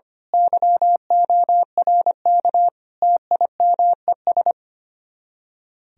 Morse code, heard as a keyed tone, about 24 wpm, 700 Hz. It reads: YORK TIMES
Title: Speed-Racing - Sets of 5 Words 25wpm